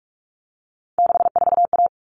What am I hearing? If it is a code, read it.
Morse code, heard as a keyed tone, about 44 wpm, 700 Hz. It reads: 64A
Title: All Cut Numbers - 3-Digit 45wpm